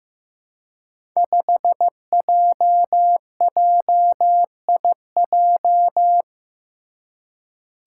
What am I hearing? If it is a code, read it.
Morse code, heard as a keyed tone, about 15 wpm, 700 Hz. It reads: 5JJIJ
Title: Five Challenging Letter-Number Combinations 15wpm